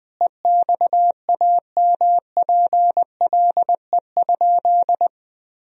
Morse code, heard as a keyed tone, about 20 wpm, 700 Hz. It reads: EXAMPLE?
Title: Sentences from Top 1000 Words - Review Part 1 - 20wpm